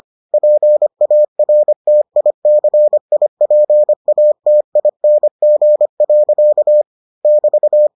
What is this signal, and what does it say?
Morse code, heard as a keyed tone, about 25 wpm, 600 Hz. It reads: PARTICIPATING. <BT>